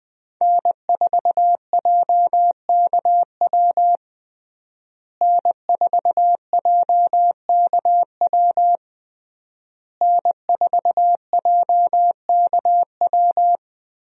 Morse code, heard as a keyed tone, about 20 wpm, 700 Hz. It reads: N4JKW N4JKW N4JKW
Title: USA Call Signs - 5 Characters - Repeated 3x - 20wpm